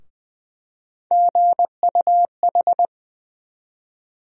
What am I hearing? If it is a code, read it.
Morse code, heard as a keyed tone, about 20 wpm, 700 Hz. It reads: GUH
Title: USA Call Sign Postfixes - 3 Letters 20wpm